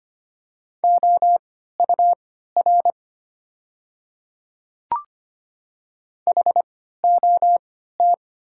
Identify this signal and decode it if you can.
Morse code, heard as a keyed tone, about 25 wpm, spaced as for 14 wpm, 700 Hz. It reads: OUR HOT
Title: 3x Character Spacing - Top 300 Words - 25wpm